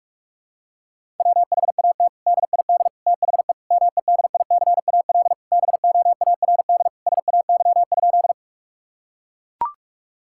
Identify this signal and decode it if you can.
Morse code, heard as a keyed tone, about 41 wpm, 700 Hz. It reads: WHAT DID THE MEDICAL BOARD SAY?
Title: Sentences from Top 1000 Words - Review Part 5 - 45wpm